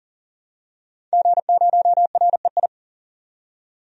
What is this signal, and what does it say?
Morse code, heard as a keyed tone, about 40 wpm, 700 Hz. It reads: G0REI
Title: DXCC-100 Call Signs - 5 Characters - 40wpm